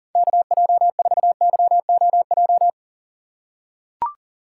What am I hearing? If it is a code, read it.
Morse code, heard as a keyed tone, about 40 wpm, 700 Hz. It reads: KJ4YOJ
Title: USA Call Signs - 6 Characters - 3x Character Spacing - 40wpm